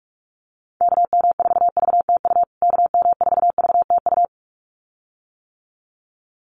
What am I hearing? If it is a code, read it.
Morse code, heard as a keyed tone, about 44 wpm, 690 Hz. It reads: KM4VTU KM4VTU
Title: USA Call Signs Repeated 2x 45wpm